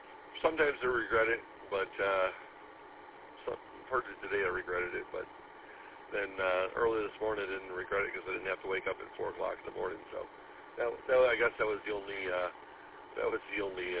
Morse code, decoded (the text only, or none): none